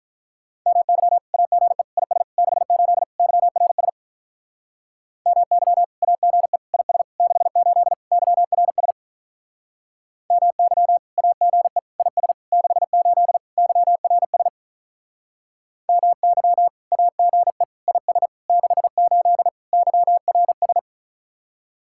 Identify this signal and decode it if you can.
Morse code, heard as a keyed tone, about 44 wpm, 700 Hz. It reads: MY AGE IS 68 YRS MY AGE IS 68 YRS MY AGE IS 68 YRS MY AGE IS 68 YRS